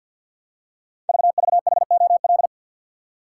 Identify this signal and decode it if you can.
Morse code, heard as a keyed tone, about 48 wpm, 700 Hz. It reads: VVHOL
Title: Code Groups - 5 Letters - 50wpm